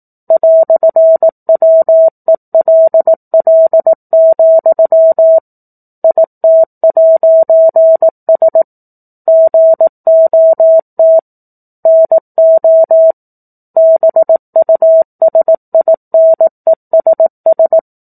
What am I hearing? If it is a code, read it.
Morse code, heard as a keyed tone, about 18 wpm, 650 Hz. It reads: "WELL, IT'S GOT NO BUSINESS
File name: 04-Chapter-04 18 WPM